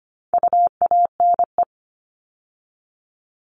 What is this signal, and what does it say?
Morse code, heard as a keyed tone, about 25 wpm, 700 Hz. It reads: UANE
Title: All Cut Numbers - 4-Digit 25wpm